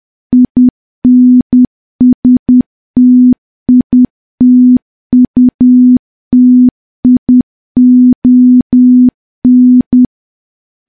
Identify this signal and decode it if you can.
Morse code, heard as a keyed tone, about 10 wpm, 250 Hz. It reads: INSTITUTION
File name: morse / morse-training-10WPM